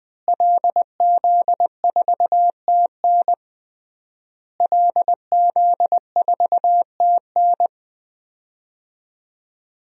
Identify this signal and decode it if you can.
Morse code, heard as a keyed tone, about 21 wpm, 700 Hz. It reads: LZ4TN LZ4TN
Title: DXCC-100 Call Signs - 5 Characters - Repeated 3x - 20wpm